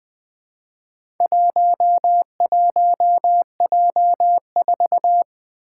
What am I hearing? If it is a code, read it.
Morse code, heard as a keyed tone, about 20 wpm, 700 Hz. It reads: 11J4
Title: Four Challenging Letter-Number Combinations 20wpm